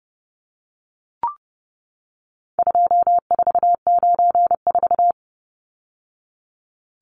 Morse code, heard as a keyed tone, about 30 wpm, 700 Hz. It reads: 2494
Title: Numbers - 4-Digit 30wpm